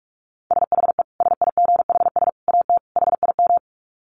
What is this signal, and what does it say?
Morse code, heard as a keyed tone, about 44 wpm, 700 Hz. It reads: SHE SIGHS AT HIM
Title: Dit Challenge 45wpm